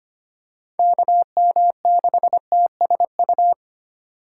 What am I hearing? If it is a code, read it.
Morse code, heard as a keyed tone, about 25 wpm, 700 Hz. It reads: KM6TSU
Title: USA Call Signs - 6 Characters - 2x Character Spacing - 25wpm